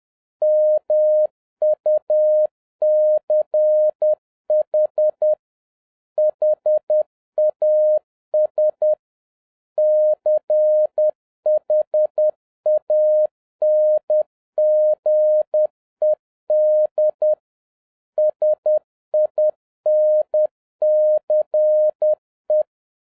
Morse code, heard as a keyed tone, about 10 wpm, 610 Hz. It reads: MUCH HAS CHANGED SINCE